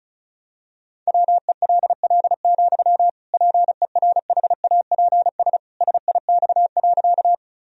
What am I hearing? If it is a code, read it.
Morse code, heard as a keyed tone, about 34 wpm, 700 Hz. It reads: WELL, PERHAPS SIX.
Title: Sentences from Top 1000 Words - Review Part 2 - 35wpm